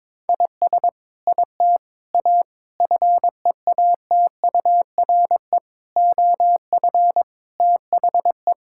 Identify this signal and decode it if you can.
Morse code, heard as a keyed tone, about 22 wpm, 700 Hz. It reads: IS IT A FEATURE OF THE
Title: Sentences from Top 1000 Words - Review Part 5 - 22wpm